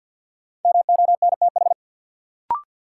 Morse code, heard as a keyed tone, about 51 wpm, 700 Hz. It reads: MONTH
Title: Sets of 5 Words Encore - New Word Drill - 50wpm